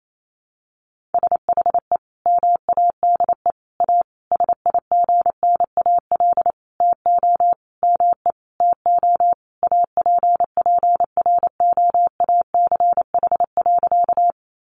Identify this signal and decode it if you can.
Morse code, heard as a keyed tone, about 28 wpm, 700 Hz. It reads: SHE MADE A SIGNAL TO ME TO APPROACH.
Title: Sentences from Top 1400 Words 28wpm